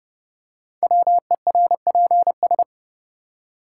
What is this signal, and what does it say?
Morse code, heard as a keyed tone, about 30 wpm, 700 Hz. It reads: WERPS
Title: Code Groups - 5 Letters - 30wpm